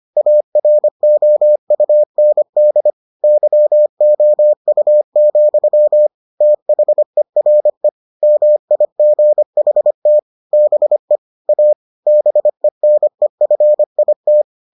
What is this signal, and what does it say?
Morse code, heard as a keyed tone, about 25 wpm, 600 Hz. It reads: AROUND YOU, THERE MIGHT BE A BENEFIT